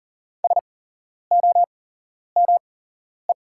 Morse code, heard as a keyed tone, about 40 wpm, spaced as for 11 wpm, 700 Hz. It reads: SOME
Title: 8x Character Spacing - Top 100 Words - 40wpm